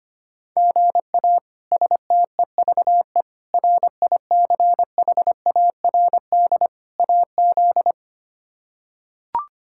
Morse code, heard as a keyed tone, about 25 wpm, 700 Hz. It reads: GA STEVE RICHARD AZ
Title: K1USN SST QSO Practice - 25wpm